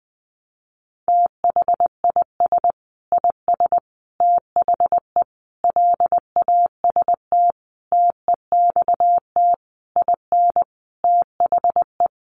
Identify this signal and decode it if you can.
Morse code, heard as a keyed tone, about 20 wpm, 700 Hz. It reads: THIS IS THE LAST TEXT IN THE
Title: Sentences from Top 1200 Words 20wpm